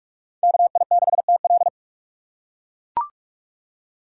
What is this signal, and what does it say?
Morse code, heard as a keyed tone, about 41 wpm, 700 Hz. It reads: KI6TL
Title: USA Call Signs - 5 Characters - 3x Character Spacing - 45wpm